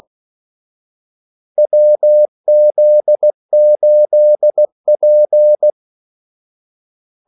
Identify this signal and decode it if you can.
Morse code, heard as a keyed tone, about 16 wpm, 600 Hz. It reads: WZ8P
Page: Kaldesignaler 16 wpm